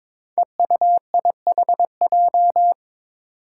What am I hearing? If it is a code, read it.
Morse code, heard as a keyed tone, about 22 wpm, 700 Hz. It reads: EUIHJ
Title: Five Challenging Letter-Number Combinations 22wpm